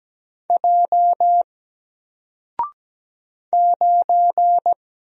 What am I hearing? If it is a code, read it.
Morse code, heard as a keyed tone, about 17 wpm, 700 Hz. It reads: J 9
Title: Single Character (Letter:Number:Punctuation:Prosign) 17wpm